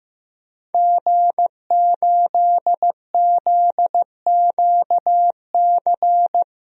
Morse code, heard as a keyed tone, about 15 wpm, 700 Hz. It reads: G8ZQC
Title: DXCC-100 Call Signs - 5 Characters - Repeated 3x - 15wpm